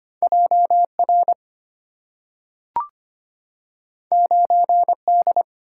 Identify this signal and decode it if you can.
Morse code, heard as a keyed tone, about 25 wpm, 690 Hz. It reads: JR 9D
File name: Two Letter-Number Combinations - 25wpm